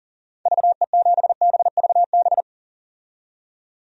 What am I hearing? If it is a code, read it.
Morse code, heard as a keyed tone, about 40 wpm, 700 Hz. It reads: VE7BVB
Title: DXCC-100 Call Signs - 6 Characters - 40wpm